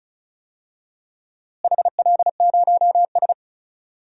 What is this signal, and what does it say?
Morse code, heard as a keyed tone, about 34 wpm, 700 Hz. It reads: HL0S
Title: DXCC-100 Call Signs - 4 Characters - Repeated 3x - 35wpm